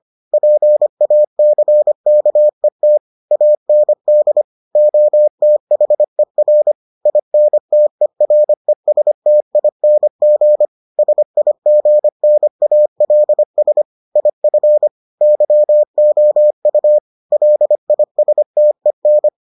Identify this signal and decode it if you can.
Morse code, heard as a keyed tone, about 25 wpm, 600 Hz. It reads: PACKET AND OTHER INTERESTING SIGNALS IF YOU LISTEN